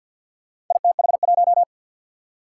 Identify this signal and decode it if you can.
Morse code, heard as a keyed tone, about 47 wpm, 700 Hz. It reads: ITH1
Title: Four Challenging Letter-Number Combinations 50wpm